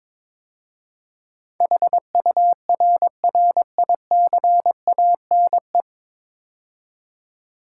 Morse code, heard as a keyed tone, about 22 wpm, 700 Hz. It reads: HURRICANE
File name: practice / 22wpm / 9 Letter Words - First 3 Letters Given - 22wpm